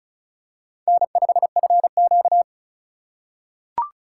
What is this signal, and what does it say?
Morse code, heard as a keyed tone, about 36 wpm, 700 Hz. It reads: N5FQ
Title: USA Call Signs - 4 Characters - 3x Character Spacing - 35wpm